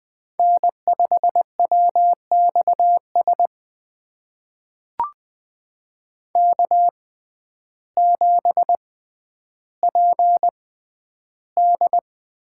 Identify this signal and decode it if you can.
Morse code, heard as a keyed tone, about 20 wpm, spaced as for 11 wpm, 700 Hz. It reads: N5WXS K7PD
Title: USA Call Signs - 5 Characters - 6x Character Spacing - 20wpm